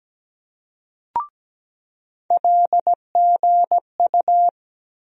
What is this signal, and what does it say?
Morse code, heard as a keyed tone, about 17 wpm, 700 Hz. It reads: LGU
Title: USA Call Sign Postfixes - 3 Letters 17wpm